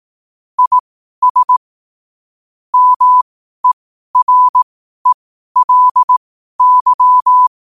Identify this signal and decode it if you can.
Morse code, heard as a keyed tone, about 18 wpm, spaced as for 12 wpm, 1000 Hz. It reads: IS MERELY
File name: qotdfiles / 12wpm